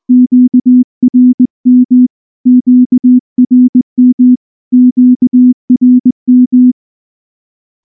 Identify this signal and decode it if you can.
Morse code, heard as a keyed tone, about 19 wpm, 250 Hz. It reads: QRM QRM QRM
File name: morse-training-20WPM